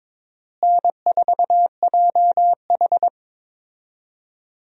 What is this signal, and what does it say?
Morse code, heard as a keyed tone, about 21 wpm, 700 Hz. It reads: N4JH
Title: USA Call Signs - 4 Characters - Repeated 3x - 22wpm